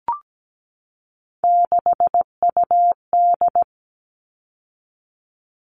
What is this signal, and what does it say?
Morse code, heard as a keyed tone, about 17 wpm, 700 Hz. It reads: 6UD
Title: All Cut Numbers - 3-Digit 17wpm